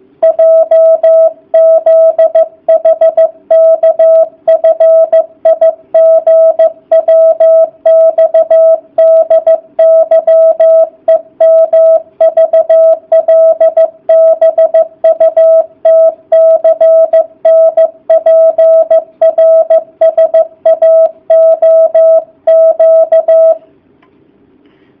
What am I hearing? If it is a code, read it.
Morse code, heard as a keyed tone, about 15 wpm, 650 Hz. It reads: JZHKFIGWXDYEMVLBUTCNPRSAOQ